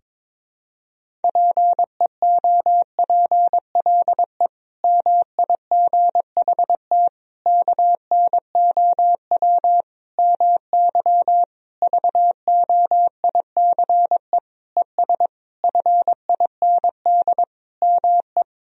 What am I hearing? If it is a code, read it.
Morse code, heard as a keyed tone, about 22 wpm, 700 Hz. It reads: PEOPLE MIGHT KNOW MY VOICE ES FIND ME
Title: Sentences from Top 500 Words and Abbreviations 22wpm